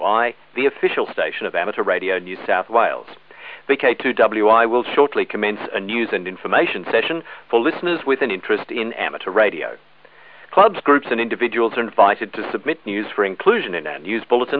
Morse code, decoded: none